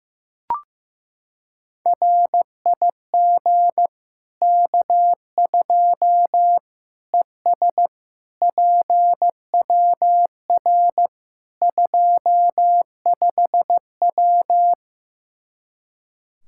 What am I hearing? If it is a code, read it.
Morse code, heard as a keyed tone, about 15 wpm, 700 Hz. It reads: RIG K2 ES PWR 25W
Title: Practice QSO - Exchange 2 - Info - 15wpm